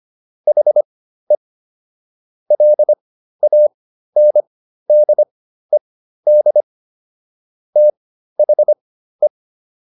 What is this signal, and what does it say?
Morse code, heard as a keyed tone, about 25 wpm, spaced as for 13 wpm, 600 Hz. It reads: HE LANDED THE